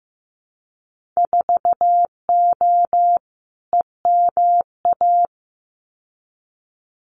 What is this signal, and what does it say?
Morse code, heard as a keyed tone, about 15 wpm, 700 Hz. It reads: 4O EMA
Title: Winter Field Day Exchanges 15wpm